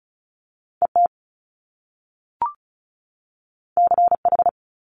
Morse code, heard as a keyed tone, about 36 wpm, 700 Hz. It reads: ET CH
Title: Top Two Letter Combinations in Words 35wpm